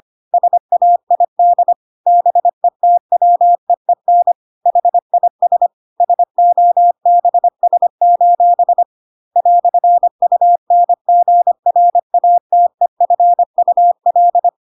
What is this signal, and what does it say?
Morse code, heard as a keyed tone, about 25 wpm, 700 Hz. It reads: SAID BETWEEN HIS SOBS: "UNGRATEFUL
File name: Pinocchio / Pinocchio ch3 25 25 ik1hge